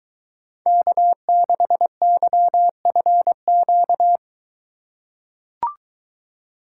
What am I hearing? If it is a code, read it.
Morse code, heard as a keyed tone, about 23 wpm, 700 Hz. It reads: K6YFQ